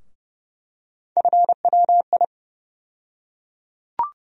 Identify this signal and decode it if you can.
Morse code, heard as a keyed tone, about 32 wpm, 700 Hz. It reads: FWI
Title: USA Call Sign Postfixes - 3 Letters 30wpm